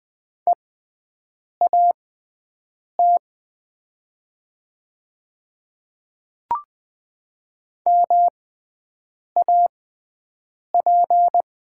Morse code, heard as a keyed tone, about 21 wpm, spaced as for 7 wpm, 700 Hz. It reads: EAT MAP
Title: 6x Character Spacing - Top 500 Words - 20wpm